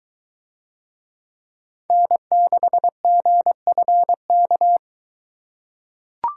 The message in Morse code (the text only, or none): N6GFK